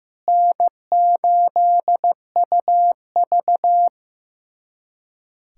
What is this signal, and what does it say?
Morse code, heard as a keyed tone, about 15 wpm, 700 Hz. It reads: N8UV
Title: USA Callsign DE Callsign - 4 Characters - 15wpm